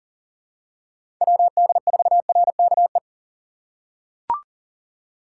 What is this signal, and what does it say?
Morse code, heard as a keyed tone, about 40 wpm, 700 Hz. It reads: WD4RKE
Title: USA Call Signs - 6 Characters - 6x Character Spacing - 40wpm